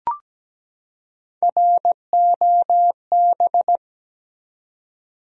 Morse code, heard as a keyed tone, about 17 wpm, 700 Hz. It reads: ROB
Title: USA Common Names - Max 3 Letters - 17wpm